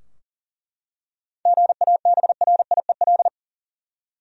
Morse code, heard as a keyed tone, about 40 wpm, 700 Hz. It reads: GABRIEL
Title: USA Common Names - 40wpm